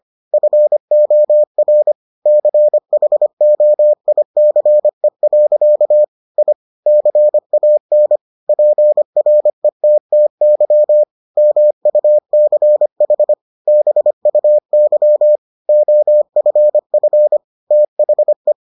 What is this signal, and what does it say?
Morse code, heard as a keyed tone, about 25 wpm, 600 Hz. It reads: FOR CHOICE. I CAN PRETTY MUCH BUY OFF THE